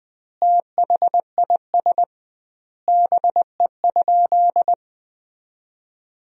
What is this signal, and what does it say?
Morse code, heard as a keyed tone, about 21 wpm, 700 Hz. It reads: THIS BE?
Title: 2x Word-Spacing - Sentences from Top 100 Words - 20wpm